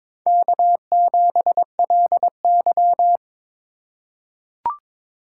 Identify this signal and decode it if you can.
Morse code, heard as a keyed tone, about 22 wpm, 700 Hz. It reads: K7LY
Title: USA Call Signs - 4 Characters - 3x Character Spacing - 22wpm